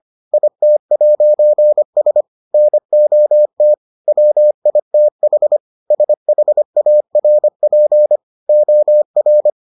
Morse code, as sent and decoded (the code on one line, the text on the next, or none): .. - .----. ... / -. --- - / .-- .. - .... / ... .... .- .-. .--. / --- .-.
IT'S NOT WITH SHARP OR